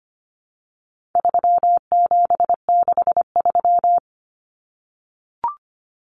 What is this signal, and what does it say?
Morse code, heard as a keyed tone, about 25 wpm, 700 Hz. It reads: 3763
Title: Common Cut Numbers - 4-Digit 25wpm